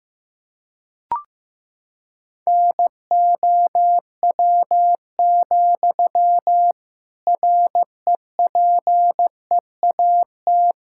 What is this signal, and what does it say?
Morse code, heard as a keyed tone, about 16 wpm, 700 Hz. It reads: NOW, REPEAT
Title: Sentences from Top 1600 Words 15wpm